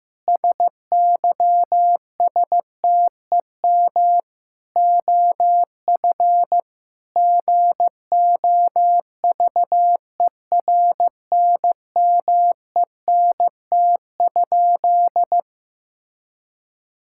Morse code, heard as a keyed tone, about 14 wpm, 700 Hz. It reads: SYSTEM OF GOVERNMENT?
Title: Sentences from Top 1000 Words - Review Part 5 - 15wpm